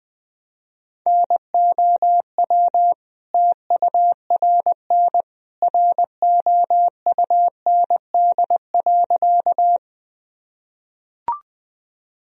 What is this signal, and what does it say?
Morse code, heard as a keyed tone, about 20 wpm, 700 Hz. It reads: NOW TURN ROUND.
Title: Sentences from Top 1000 Words - Review Part 1 - 20wpm